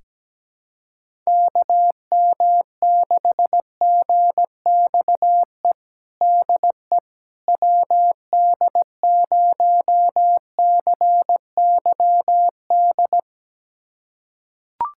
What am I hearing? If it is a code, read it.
Morse code, heard as a keyed tone, about 17 wpm, 700 Hz. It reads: KM6GXE DE WD0CYD
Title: USA Callsign DE Callsign - 6 Characters - 17wpm